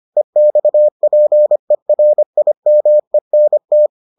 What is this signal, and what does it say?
Morse code, heard as a keyed tone, about 25 wpm, 600 Hz. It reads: EXPERIMENT